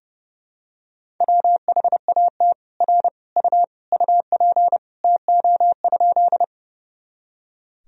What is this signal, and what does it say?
Morse code, heard as a keyed tone, about 30 wpm, 700 Hz. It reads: WHAT R U UP TO?
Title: Sentences from Top 100 Words and Abbreviations 30wpm